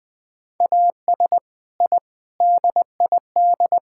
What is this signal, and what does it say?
Morse code, heard as a keyed tone, about 20 wpm, 700 Hz. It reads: AS I DID
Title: Sentences from Top 1000 Words - Review Part 6 - 20wpm